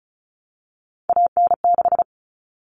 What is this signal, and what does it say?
Morse code, heard as a keyed tone, about 34 wpm, 700 Hz. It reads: AN6
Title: Common Cut Numbers - 3-Digit 35wpm